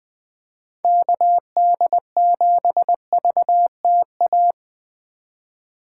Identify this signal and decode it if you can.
Morse code, heard as a keyed tone, about 20 wpm, 700 Hz. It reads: KD7VTA
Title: USA Call Signs - 6 Characters - 4x Character Spacing - 20wpm